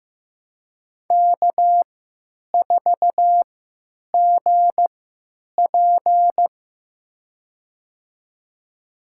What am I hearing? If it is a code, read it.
Morse code, heard as a keyed tone, about 15 wpm, spaced as for 9 wpm, 700 Hz. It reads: K4GP
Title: USA Call Signs - 4 Characters - 3x Character Spacing - 15wpm